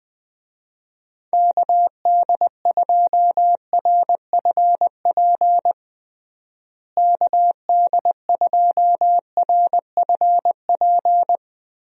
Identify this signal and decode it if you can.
Morse code, heard as a keyed tone, about 20 wpm, 700 Hz. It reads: KD2RFP KD2RFP
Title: USA Call Signs - 6 Characters - Repeated 3x - 20wpm